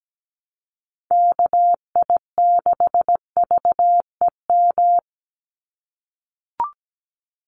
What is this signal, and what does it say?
Morse code, heard as a keyed tone, about 17 wpm, 700 Hz. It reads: KI6VEM
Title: USA Call Signs 17wpm